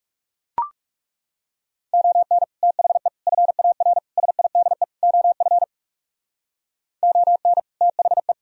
Morse code, heard as a keyed tone, about 36 wpm, 700 Hz. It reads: ON THE FAR SIDE OF ON THE
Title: Speed-Racing - Sets of 5 Words 30wpm